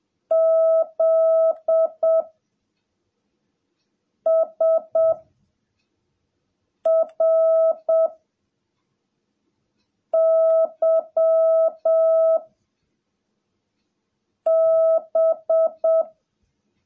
Morse code, heard as a keyed tone, about 7 wpm, 660 Hz. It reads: Z S R Y B
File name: jh8cbh / 2021cw / cw17a